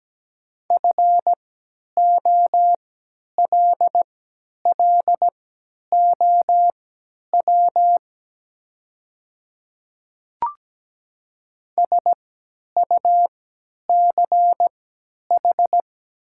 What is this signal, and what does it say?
Morse code, heard as a keyed tone, about 17 wpm, spaced as for 10 wpm, 700 Hz. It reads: FOLLOW SUCH
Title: 3x Character Spacing - Top 300 Words - 17wpm